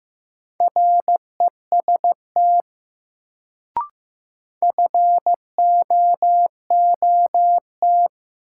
Morse code, heard as a keyed tone, about 15 wpm, 700 Hz. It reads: REST FOOT
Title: Sentences from Top 500 Words - New Word Drill - 15wpm